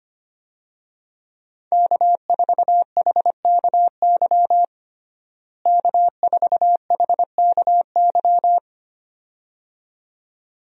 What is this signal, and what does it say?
Morse code, heard as a keyed tone, about 25 wpm, 700 Hz. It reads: K4HKY K4HKY
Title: USA Call Signs - 5 Characters - Repeated 2x - 25wpm